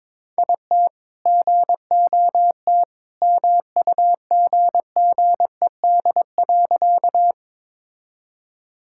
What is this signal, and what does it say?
Morse code, heard as a keyed tone, about 22 wpm, 700 Hz. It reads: IT GOT MUGGED.
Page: morse-code-ninja-practice-22wpm